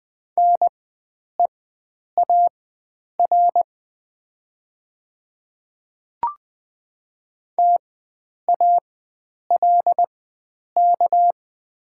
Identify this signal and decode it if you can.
Morse code, heard as a keyed tone, about 20 wpm, spaced as for 9 wpm, 700 Hz. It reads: NEAR TALK
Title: 4x Character Spacing - Top 500 Words - 20wpm